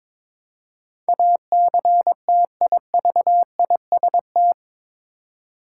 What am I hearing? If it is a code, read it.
Morse code, heard as a keyed tone, about 22 wpm, 700 Hz. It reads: ACTIVIST
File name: practice / 22wpm / Eight Letter Words - 22wpm